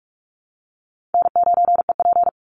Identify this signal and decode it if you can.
Morse code, heard as a keyed tone, about 46 wpm, 700 Hz. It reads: N9EP
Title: Easy Call Signs - 45wpm